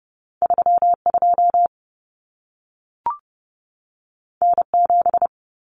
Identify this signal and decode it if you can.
Morse code, heard as a keyed tone, about 30 wpm, 700 Hz. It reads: 32 N7
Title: Common Cut Numbers - 2-Digit 30wpm